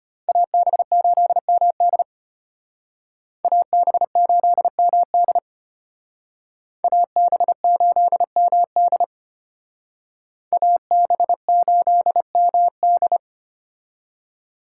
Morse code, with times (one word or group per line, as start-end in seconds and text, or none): 0.3-2.0 AB8MD
3.4-5.4 AB8MD
6.8-9.0 AB8MD
10.5-13.2 AB8MD